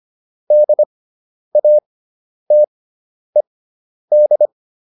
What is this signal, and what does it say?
Morse code, heard as a keyed tone, about 25 wpm, spaced as for 10 wpm, 600 Hz. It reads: DATED
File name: ED5 10